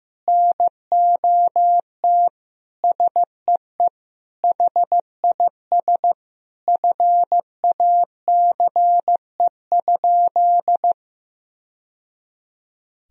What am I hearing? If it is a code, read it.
Morse code, heard as a keyed tone, about 15 wpm, 700 Hz. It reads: NOT SEE HIS FACE?
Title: Sentences from Top 1000 Words - Review Part 2 - 15wpm